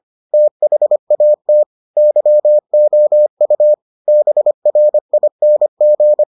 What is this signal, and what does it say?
Morse code, heard as a keyed tone, about 25 wpm, 600 Hz. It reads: THAT YOU BRING